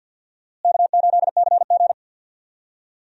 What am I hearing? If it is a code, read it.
Morse code, heard as a keyed tone, about 52 wpm, 700 Hz. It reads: K8CG
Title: USA Call Signs - 4 Characters - Repeated 3x - 50wpm